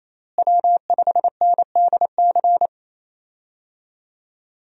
Morse code, heard as a keyed tone, about 28 wpm, 700 Hz. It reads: W5NDC
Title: USA Call Signs - 5 Characters - Repeated 3x - 28wpm